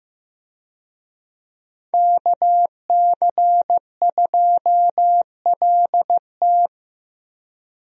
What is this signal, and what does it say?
Morse code, heard as a keyed tone, about 15 wpm, 700 Hz. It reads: KC2LT
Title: USA Call Signs - 5 Characters - Repeated 2x - 15wpm